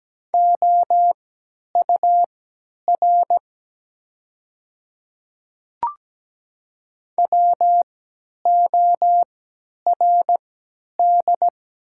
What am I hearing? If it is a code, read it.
Morse code, heard as a keyed tone, about 17 wpm, spaced as for 10 wpm, 700 Hz. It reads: OUR WORD
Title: 3x Character Spacing - Top 200 Words - 17wpm